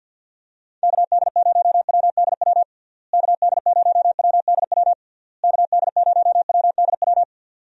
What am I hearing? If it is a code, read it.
Morse code, heard as a keyed tone, about 50 wpm, 700 Hz. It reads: KD0WDW KD0WDW KD0WDW
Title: USA Call Signs - 6 Characters - Repeated 3x - 50wpm